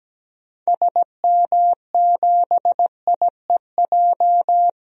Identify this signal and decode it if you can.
Morse code, heard as a keyed tone, about 17 wpm, 700 Hz. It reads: SM7IEJ
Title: DXCC-100 Call Signs - 6 Characters - 17wpm